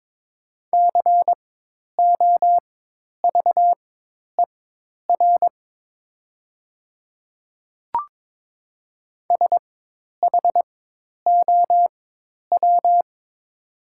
Morse code, heard as a keyed tone, about 22 wpm, spaced as for 10 wpm, 700 Hz. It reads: COVER SHOW
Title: 4x Character Spacing - Top 500 Words - 22wpm